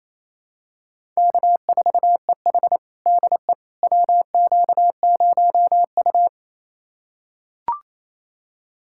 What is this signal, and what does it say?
Morse code, heard as a keyed tone, about 28 wpm, 700 Hz. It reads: K4EH DE WQ0U
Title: USA Callsign DE Callsign - 4 Characters - 28wpm